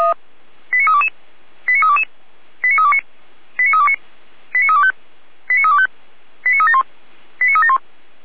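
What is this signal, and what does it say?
Five-tone selective call (ZVEI), heard as quick runs of five tones, tones of about 70 ms. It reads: 89322 89322 89329 89329 89325 89325 89371 89371
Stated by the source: Received on SDR; station feuerwehr-orlamuende.org (fire brigade Orlamünde).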